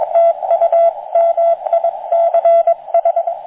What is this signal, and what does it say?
Morse code, heard as a keyed tone, about 21 wpm, 680 Hz, with radio noise. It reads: TUMICS